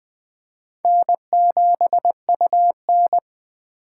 Morse code, heard as a keyed tone, about 20 wpm, 700 Hz. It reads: N7UN
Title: 2023 SOTA Chaser Call Signs - 20wpm